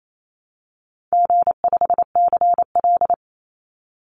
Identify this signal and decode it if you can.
Morse code, heard as a keyed tone, about 28 wpm, 700 Hz. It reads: G5CL
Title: Easy Call Signs - 28wpm